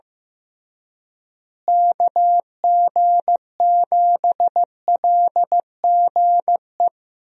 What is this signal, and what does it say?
Morse code, heard as a keyed tone, about 15 wpm, 700 Hz. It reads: KG7LGE
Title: USA Call Signs - 6 Characters - Repeated 3x - 15wpm